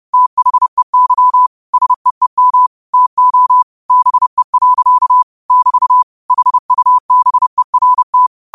Morse code, heard as a keyed tone, about 30 wpm, 1000 Hz. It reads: THEY SEEM TO BE. <BT> HUBERT
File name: qotdfiles / 30wpm